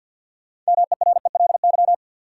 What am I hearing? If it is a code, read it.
Morse code, heard as a keyed tone, about 52 wpm, 700 Hz. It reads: MERELY
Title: Sentences from Top 1400 Words - New Word Drill Alt - 50wpm